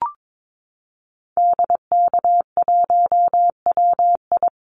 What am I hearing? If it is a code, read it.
Morse code, heard as a keyed tone, about 22 wpm, 700 Hz. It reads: DK1WI
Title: Call Signs - 22wpm